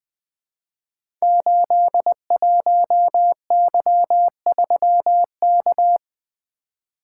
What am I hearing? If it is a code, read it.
Morse code, heard as a keyed tone, about 20 wpm, 700 Hz. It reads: 81Y3K
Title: Code Groups - 5 Letter:Numbers - 20wpm